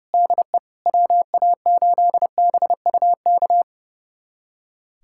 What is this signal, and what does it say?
Morse code, heard as a keyed tone, about 30 wpm, 700 Hz. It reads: DE WA8BUK
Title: USA Callsign DE Callsign - 6 Characters - 30wpm